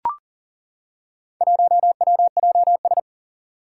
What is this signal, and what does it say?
Morse code, heard as a keyed tone, about 40 wpm, 690 Hz. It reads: 1WJS
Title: Four Challenging Letter-Number Combinations 40wpm